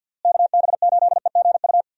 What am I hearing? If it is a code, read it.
Morse code, heard as a keyed tone, about 50 wpm, 700 Hz. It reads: KB8EGU
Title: USA Call Signs - 6 Characters - Repeated 3x - 50wpm